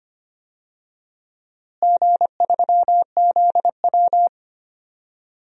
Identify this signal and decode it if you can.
Morse code, heard as a keyed tone, about 25 wpm, 700 Hz. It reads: G3ZW
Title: DXCC-100 Call Signs - 4 Characters - Repeated 3x - 25wpm